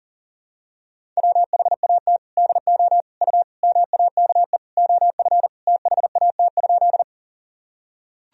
Morse code, heard as a keyed tone, about 40 wpm, 700 Hz. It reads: WHAT DO U MAKE OF THAT?
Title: Sentences from Top 100 Words and Abbreviations 40wpm